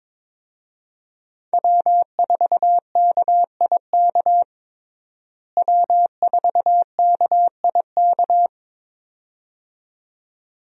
Morse code, heard as a keyed tone, about 22 wpm, 700 Hz. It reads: W4KIK W4KIK
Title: USA Call Signs - 5 Characters - Repeated 2x - 22wpm